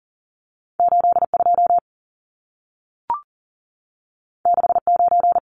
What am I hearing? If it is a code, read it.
Morse code, heard as a keyed tone, about 40 wpm, 700 Hz. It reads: 82 69
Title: Numbers - 2-Digit 40wpm